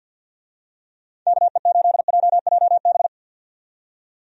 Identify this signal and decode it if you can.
Morse code, heard as a keyed tone, about 49 wpm, 700 Hz. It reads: KE8JJB
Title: USA Call Signs - 6 Characters - 50wpm